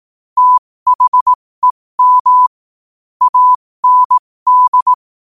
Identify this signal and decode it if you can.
Morse code, heard as a keyed tone, about 18 wpm, spaced as for 15 wpm, 1000 Hz. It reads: THEM AND
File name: qotdfiles / 15wpm